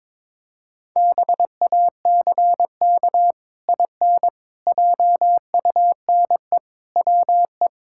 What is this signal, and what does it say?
Morse code, heard as a keyed tone, about 22 wpm, 700 Hz. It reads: BACK IN JUNE WE
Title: Pangram Challenge v2 - 22wpm